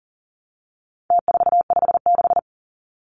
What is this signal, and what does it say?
Morse code, heard as a keyed tone, about 40 wpm, 700 Hz. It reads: T456
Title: Numbers - Contest Style - 4-Digit 40wpm